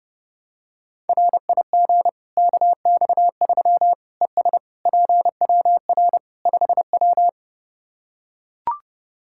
Morse code, heard as a keyed tone, about 31 wpm, 700 Hz. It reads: RIG KX3 ES PWR 5W
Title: Practice QSO - Exchange 2 - Info - 30wpm